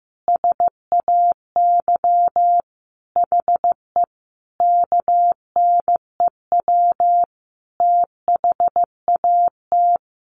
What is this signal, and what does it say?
Morse code, heard as a keyed tone, about 15 wpm, 700 Hz. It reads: SAY HE KNEW THAT
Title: Sentences from Top 1500 Words 15wpm